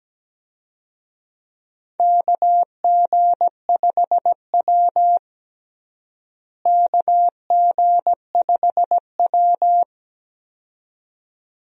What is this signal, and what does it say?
Morse code, heard as a keyed tone, about 17 wpm, 700 Hz. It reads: KG5W KG5W
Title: USA Call Signs - 4 Characters - Repeated 2x - 17wpm